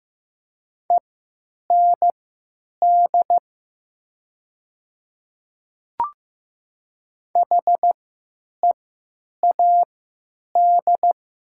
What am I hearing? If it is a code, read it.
Morse code, heard as a keyed tone, about 15 wpm, spaced as for 9 wpm, 700 Hz. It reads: END HEAD
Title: 3x Character Spacing - Top 500 Words - 15wpm